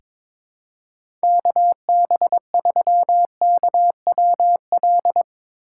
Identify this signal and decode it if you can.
Morse code, heard as a keyed tone, about 23 wpm, 700 Hz. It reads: KB3KWL
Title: USA Call Signs - 2x Character Spacing - 22wpm